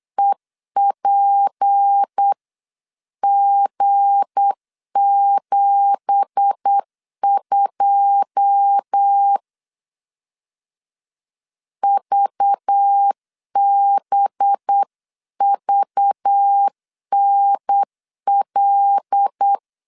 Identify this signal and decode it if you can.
Morse code, heard as a keyed tone, about 8 wpm, 800 Hz. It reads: EPG72 VBVNL
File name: 8wpm